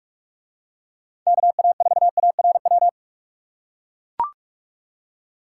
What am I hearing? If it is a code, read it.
Morse code, heard as a keyed tone, about 44 wpm, 700 Hz. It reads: KA4ARW